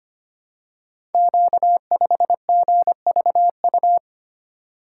Morse code, heard as a keyed tone, about 25 wpm, 700 Hz. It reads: Q5GVU
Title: Code Groups - 5 Letter:Numbers - 25wpm